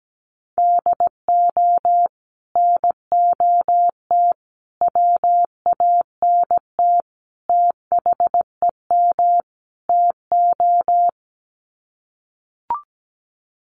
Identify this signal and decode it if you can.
Morse code, heard as a keyed tone, about 17 wpm, 700 Hz. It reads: DO NOT WANT THEM TO
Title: Sets of 5 Words Encore 17wpm